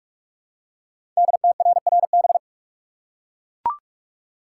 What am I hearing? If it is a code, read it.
Morse code, heard as a keyed tone, about 44 wpm, 700 Hz. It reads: DTRRB
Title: Code Groups - 5 Letters - 45wpm